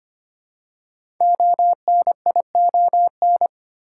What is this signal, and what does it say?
Morse code, heard as a keyed tone, about 25 wpm, 700 Hz. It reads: ONION
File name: practice / 25wpm / Difficult Words with Hints - 25wpm